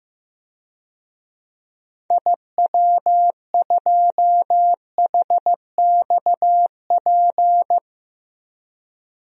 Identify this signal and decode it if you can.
Morse code, heard as a keyed tone, about 16 wpm, 700 Hz. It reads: IW2HXP
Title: DXCC-100 Call Signs - 6 Characters - 15wpm